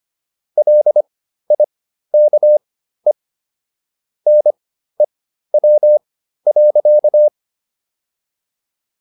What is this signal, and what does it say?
Morse code, heard as a keyed tone, about 25 wpm, spaced as for 13 wpm, 600 Hz. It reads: LIKE NEW.